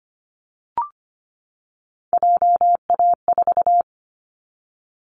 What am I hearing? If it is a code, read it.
Morse code, heard as a keyed tone, about 25 wpm, 700 Hz. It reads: JA4
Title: Callsign Prefixes for Top 25 DXCC - 25wpm